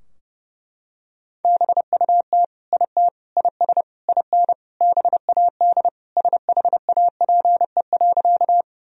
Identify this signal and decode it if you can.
Morse code, heard as a keyed tone, about 30 wpm, 700 Hz. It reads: BUT IT IS IN BAD SHAPE.
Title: Sentences from Top 1000 Words - Review Part 3 - 30wpm